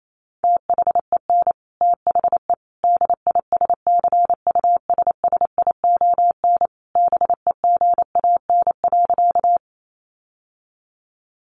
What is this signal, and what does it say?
Morse code, heard as a keyed tone, about 28 wpm, 700 Hz. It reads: THEN THE DISCUSSION BEGAN.
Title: Sentences from Top 1100 Words 28wpm